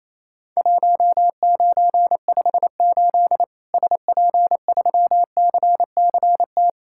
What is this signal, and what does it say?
Morse code, heard as a keyed tone, about 28 wpm, 700 Hz. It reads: 1958 SP3CCT
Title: Worked All Europe DX Contest - QTC Exchange - 28wpm